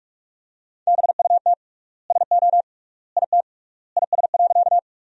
Morse code, heard as a keyed tone, about 46 wpm, 700 Hz. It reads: BUT SO IT IS.
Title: 3x Word-Spacing - Sentences from Top 100 Words - 45wpm